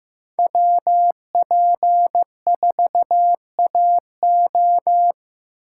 Morse code, heard as a keyed tone, about 15 wpm, 700 Hz. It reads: WP4AO
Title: USA Call Signs - 5 Characters - 2x Character Spacing - 15wpm